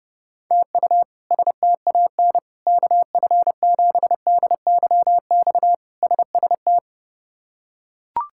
Morse code, heard as a keyed tone, about 30 wpm, 700 Hz. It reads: TU STAN KF7DYX SST
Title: K1USN SST QSO Practice - 30wpm